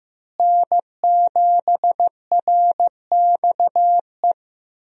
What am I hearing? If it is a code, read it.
Morse code, heard as a keyed tone, about 15 wpm, 700 Hz. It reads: N7RXE
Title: USA Call Signs - 5 Characters - Repeated 2x - 15wpm